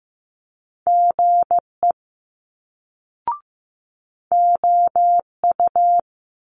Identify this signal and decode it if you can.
Morse code, heard as a keyed tone, about 15 wpm, 700 Hz. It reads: GE OU
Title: Multi-letter Phonemes Quick Fire 15wpm